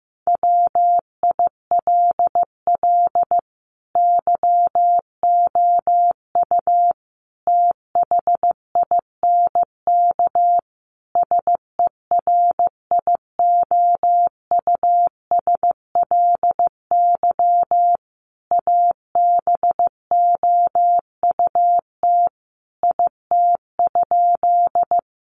Morse code, heard as a keyed tone, about 15 wpm, 700 Hz. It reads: WILL YOU THINK SERIOUSLY ABOUT IT?
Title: Sentences from Top 1200 Words 15wpm